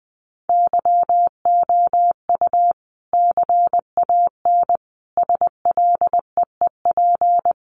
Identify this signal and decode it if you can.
Morse code, heard as a keyed tone, about 20 wpm, 700 Hz. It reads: YOU CAN SLEEP
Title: Sentences from Top 500 Words 20wpm